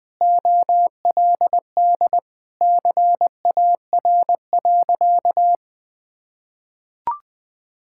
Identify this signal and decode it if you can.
Morse code, heard as a keyed tone, about 20 wpm, 700 Hz. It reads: OLD CAR.
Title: Sentences with 8-Word Phrases - 20wpm